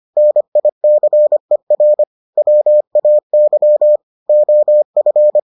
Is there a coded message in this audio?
Yes: Morse code from 0.2 to 5.4 s, heard as a keyed tone, about 25 wpm, 600 Hz. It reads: NICER WAY OF